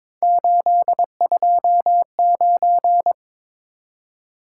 Morse code, heard as a keyed tone, about 22 wpm, 700 Hz. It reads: 829
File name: practice / 22wpm / Three Letter-Number Combinations - 22wpm